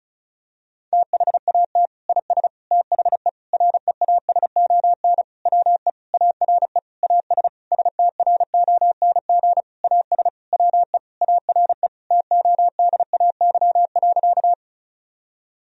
Morse code, heard as a keyed tone, about 34 wpm, 700 Hz. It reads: THAT IS THE REASON WE ARE AS STRONG AS WE ARE TODAY.
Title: Sentences from Top 1000 Words - Review Part 4 - 35wpm